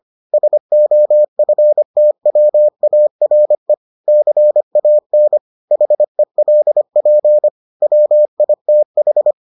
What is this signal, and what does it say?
Morse code, heard as a keyed tone, about 25 wpm, 600 Hz. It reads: SOFTWARE CAN HELP WITH